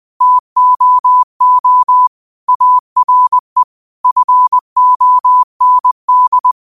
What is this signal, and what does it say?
Morse code, heard as a keyed tone, about 20 wpm, 1000 Hz. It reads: TOO ARE FOND